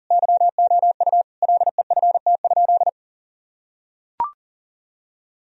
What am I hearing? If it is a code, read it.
Morse code, heard as a keyed tone, about 40 wpm, 700 Hz. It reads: YOU LEFT?
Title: Sentences from Top 1000 Words - Review Part 2 - 40wpm